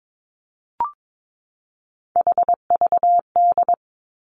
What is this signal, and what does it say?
Morse code, heard as a keyed tone, about 22 wpm, 700 Hz. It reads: HVD
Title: Three Challenging Letter-Number Combinations 22wpm